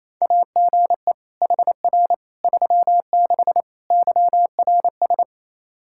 Morse code, heard as a keyed tone, about 28 wpm, 700 Hz. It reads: AGE HR 36 YRS
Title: Practice QSO - Exchange 3 - More Info - 28wpm